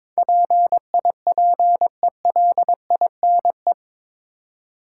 Morse code, heard as a keyed tone, about 22 wpm, 700 Hz. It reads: PIPELINE
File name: practice / 22wpm / Eight Letter Words - 22wpm